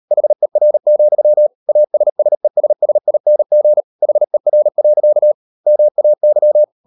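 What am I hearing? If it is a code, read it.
Morse code, heard as a keyed tone, about 38 wpm, 600 Hz. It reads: HER, ASSESSING HER. MAY
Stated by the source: Morse code practice at 38 WPM, sending Ken Follett's novel Never in English, from CW med Gnister.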